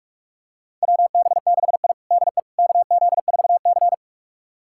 Morse code, heard as a keyed tone, about 41 wpm, 700 Hz. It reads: WB6I DE KZ4C
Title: USA Callsign DE Callsign - 4 Characters - 45wpm